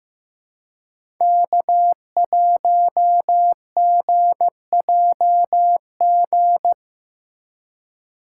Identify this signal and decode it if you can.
Morse code, heard as a keyed tone, about 15 wpm, 700 Hz. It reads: K1GJG
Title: USA Call Signs - 2x Character Spacing - 15wpm